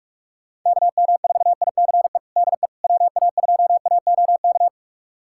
Morse code, heard as a keyed tone, about 42 wpm, 700 Hz. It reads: KM4ICE DE WA2AOK
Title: USA Callsign DE Callsign - 6 Characters - 45wpm